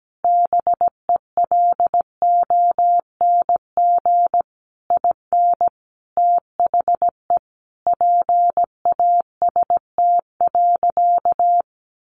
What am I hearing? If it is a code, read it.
Morse code, heard as a keyed tone, about 17 wpm, 700 Hz. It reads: BELONG IN THE PAST.
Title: A Sound Of Thunder 17wpm